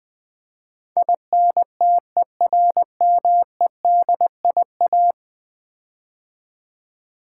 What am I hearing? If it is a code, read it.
Morse code, heard as a keyed tone, about 20 wpm, 700 Hz. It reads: INTERMEDIA
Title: Build A Word - 5 Letters At A Time - 12-Letter Words - 20wpm